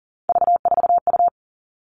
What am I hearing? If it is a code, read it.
Morse code, heard as a keyed tone, about 40 wpm, 700 Hz. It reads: V4U
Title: All Cut Numbers - 3-Digit 40wpm